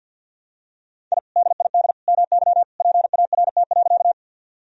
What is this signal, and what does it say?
Morse code, heard as a keyed tone, about 52 wpm, 700 Hz. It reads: I DID MY PART.
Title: Sentences from Top 1000 Words - Review Part 1 - 50wpm